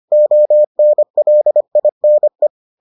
Morse code, heard as a keyed tone, about 25 wpm, 600 Hz. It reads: ONLINE